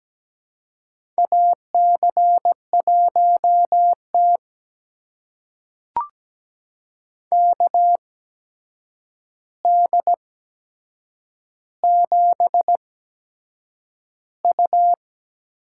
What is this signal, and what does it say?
Morse code, heard as a keyed tone, about 17 wpm, spaced as for 7 wpm, 700 Hz. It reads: AC1T KD7U
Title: USA Call Signs - 8x Character Spacing - 17wpm